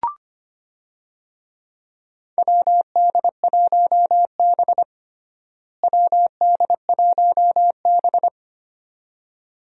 Morse code, heard as a keyed tone, about 25 wpm, 700 Hz. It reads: WD1B WD1B
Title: USA Call Signs - 4 Characters - Repeated 2x - 25wpm